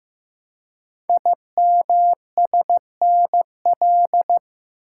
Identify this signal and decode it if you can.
Morse code, heard as a keyed tone, about 15 wpm, 700 Hz. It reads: IMSNL
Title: Code Groups - 5 Letters - 15wpm